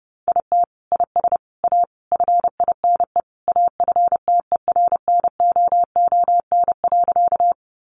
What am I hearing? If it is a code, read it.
Morse code, heard as a keyed tone, about 30 wpm, 700 Hz. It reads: IT IS A FINE AFTERNOON.
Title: Sentences from Top 1500 Words 30wpm